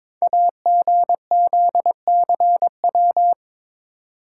Morse code, heard as a keyed tone, about 22 wpm, 700 Hz. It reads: AGZCW
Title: Code Groups - 5 Letters - 22wpm